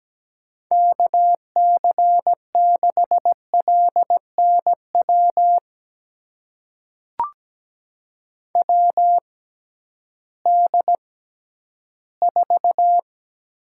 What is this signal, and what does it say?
Morse code, heard as a keyed tone, about 17 wpm, spaced as for 11 wpm, 700 Hz. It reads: KC6LNW WD4
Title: USA Call Signs - 6 Characters - 6x Character Spacing - 17wpm